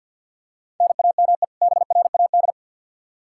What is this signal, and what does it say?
Morse code, heard as a keyed tone, about 50 wpm, 700 Hz. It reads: NAME BRAD
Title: Practice QSO - Exchange 1 - Beginning - 50wpm